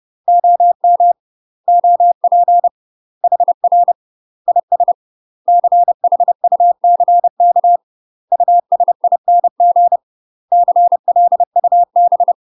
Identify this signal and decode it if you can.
Morse code, heard as a keyed tone, about 30 wpm, 700 Hz. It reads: OM OP HR IS CHUCK USING CLUB
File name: UK MASTER QSOs TRAINING / 30WPM / UK MASTER QSOs 0035